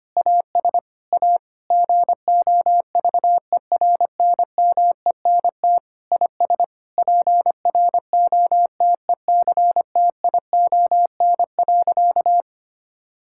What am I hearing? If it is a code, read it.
Morse code, heard as a keyed tone, about 25 wpm, 700 Hz. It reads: AS A GOVERNMENT IS PROTECTION.
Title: Sentences from Top 1600 Words 25wpm